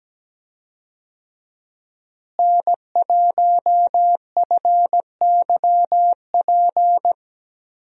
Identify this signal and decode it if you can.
Morse code, heard as a keyed tone, about 17 wpm, 700 Hz. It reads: N1FYP
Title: USA Call Signs - 5 Characters - Repeated 2x - 17wpm